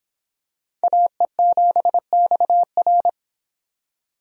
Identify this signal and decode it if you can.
Morse code, heard as a keyed tone, about 25 wpm, 700 Hz. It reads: AE7XR